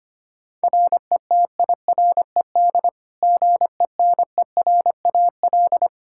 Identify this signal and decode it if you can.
Morse code, heard as a keyed tone, about 25 wpm, 700 Hz. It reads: RETIRED GENERAL
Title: Practice QSO - Exchange 3 - More Info - 25wpm